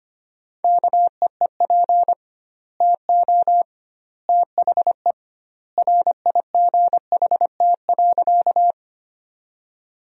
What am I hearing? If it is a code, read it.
Morse code, heard as a keyed tone, about 25 wpm, 700 Hz. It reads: KEEP TO THE RIGHT.
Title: 2x Word-Spacing - Sentences from Top 300 Words - 25wpm